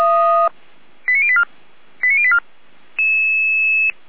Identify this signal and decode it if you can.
Five-tone selective call (ZVEI), heard as quick runs of five tones, tones of about 70 ms. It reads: 89084 89084